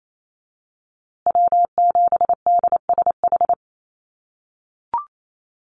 Morse code, heard as a keyed tone, about 28 wpm, 690 Hz. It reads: W7DSH
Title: USA Call Signs Repeated 3x 28wpm